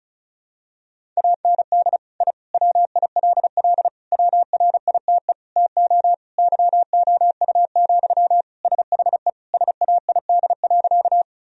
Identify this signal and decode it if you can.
Morse code, heard as a keyed tone, about 36 wpm, 700 Hz. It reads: AND I WILL WRITE TO YOU, SHE SAID.
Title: Sentences from Top 1000 Words - Review Part 1 - 35wpm